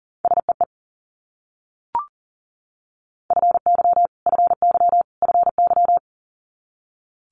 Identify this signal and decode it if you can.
Morse code, heard as a keyed tone, about 42 wpm, 700 Hz. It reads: SEE FY FY FY
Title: Multi-letter Phonemes Repeated 3x 40wpm